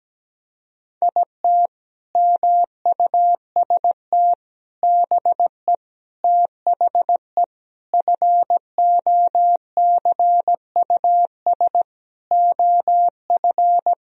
Morse code, heard as a keyed tone, about 17 wpm, 700 Hz. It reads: IT MUST BE THE FOCUS OF
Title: Sentences from Top 1000 Words - Review Part 4 - 17wpm